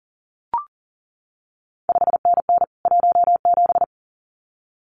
Morse code, heard as a keyed tone, about 41 wpm, 700 Hz. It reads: 5NN 17
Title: CQ WW DX Contest Exchange Encore - 40wpm